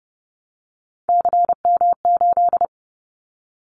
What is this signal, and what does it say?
Morse code, heard as a keyed tone, about 30 wpm, 700 Hz. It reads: CM8
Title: Callsign Prefixes for Top 100 DXCC - 30wpm